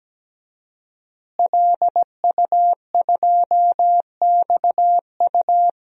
Morse code, heard as a keyed tone, about 17 wpm, 700 Hz. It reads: LU2XU
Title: DXCC-100 Call Signs - 5 Characters - Repeated 2x - 17wpm